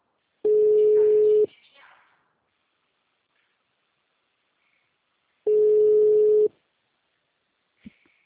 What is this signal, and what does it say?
Ringback tone (Europe, 425 Hz), with speech.
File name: out- -2001- - - .90